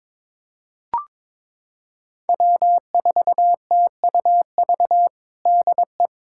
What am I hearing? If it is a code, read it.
Morse code, heard as a keyed tone, about 22 wpm, 700 Hz. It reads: W4TUV DE
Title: USA Callsign DE Callsign - 5 Characters - 22wpm